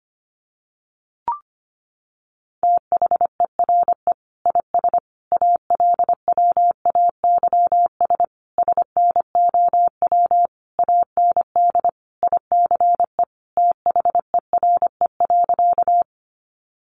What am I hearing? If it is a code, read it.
Morse code, heard as a keyed tone, about 25 wpm, 700 Hz. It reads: THERE IS ALWAYS SNOW AND ICE THERE.
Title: Sentences from Top 1000 Words 25wpm